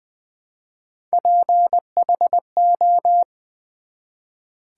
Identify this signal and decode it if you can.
Morse code, heard as a keyed tone, about 20 wpm, 700 Hz. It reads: PHO
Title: Build A Word - 1 Letter At A Time - 5-Letter Words - 20wpm